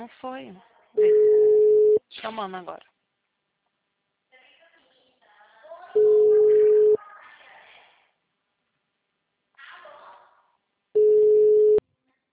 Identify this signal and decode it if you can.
Ringback tone (Europe, 425 Hz), with speech.